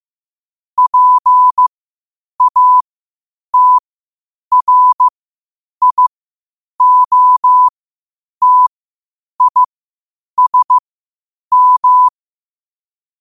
Morse code, heard as a keyed tone, about 15 wpm, spaced as for 8 wpm, 1000 Hz. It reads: PATRIOTISM
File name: qotdfiles / 8wpm